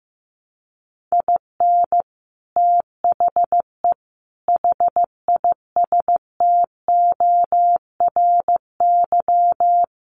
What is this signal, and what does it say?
Morse code, heard as a keyed tone, about 15 wpm, 700 Hz. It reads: IN THE HISTORY
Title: Sets of 4 Words 15wpm